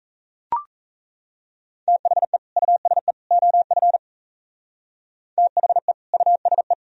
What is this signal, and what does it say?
Morse code, heard as a keyed tone, about 34 wpm, 700 Hz. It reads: THE USE OF THE USE
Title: Speed-Racing - Sets of 3 Words 28wpm